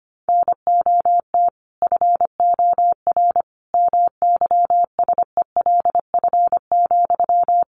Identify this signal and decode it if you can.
Morse code, heard as a keyed tone, about 25 wpm, 700 Hz. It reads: NOT FOR MYSELF,
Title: Sentences from Top 1300 Words 25wpm